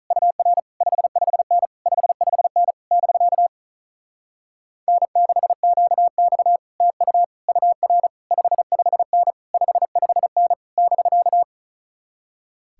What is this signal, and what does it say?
Morse code, heard as a keyed tone, about 44 wpm, 700 Hz. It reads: UR 55N 55N <BK> N6QX TU UR 55N 55N <BK>